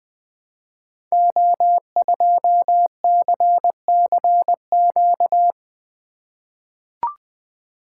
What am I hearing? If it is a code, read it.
Morse code, heard as a keyed tone, about 20 wpm, 700 Hz. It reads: O2CCQ
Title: Code Groups - 5 Letter:Numbers - 20wpm